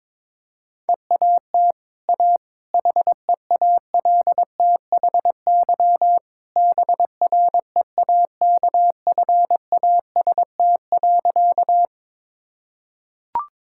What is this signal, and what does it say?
Morse code, heard as a keyed tone, about 21 wpm, 690 Hz. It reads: EAT A HEALTHY BREAKFAST.
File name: practice / 22wpm / Easy Sentences with Hints - 22wpm